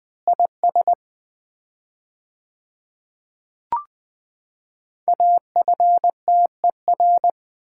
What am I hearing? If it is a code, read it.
Morse code, heard as a keyed tone, about 20 wpm, 700 Hz. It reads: IS AFTER
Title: 4x Word-Spacing - Sets of 2 Words - 20wpm